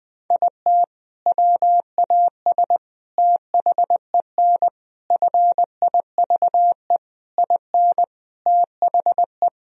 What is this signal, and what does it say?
Morse code, heard as a keyed tone, about 20 wpm, 700 Hz. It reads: IT WAS THEN FIVE IN THE
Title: Sentences from Top 1000 Words - Review Part 2 - 20wpm